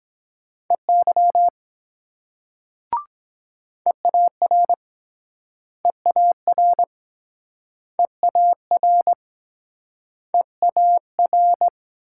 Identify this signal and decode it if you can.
Morse code, heard as a keyed tone, about 23 wpm, 700 Hz. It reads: EY EAR EAR EAR EAR